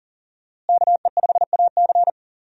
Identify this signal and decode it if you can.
Morse code, heard as a keyed tone, about 40 wpm, 700 Hz. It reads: KE5AC
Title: USA Call Signs - 5 Characters - Repeated 2x - 40wpm